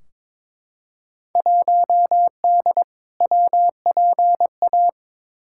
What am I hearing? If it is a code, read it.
Morse code, heard as a keyed tone, about 22 wpm, 700 Hz. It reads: 1D WPA
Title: Field Day Exchanges - 22wpm